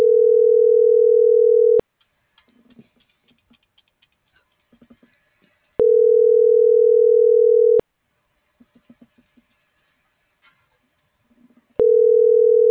Ringback tone (North America), 440 + 480 Hz.